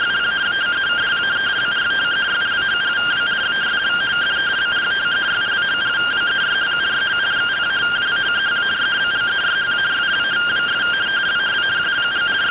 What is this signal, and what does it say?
Two-tone teleprinter signal, about 170 Hz shift.
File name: findings / axial iodine 10mm 184.gen-30 rtty rx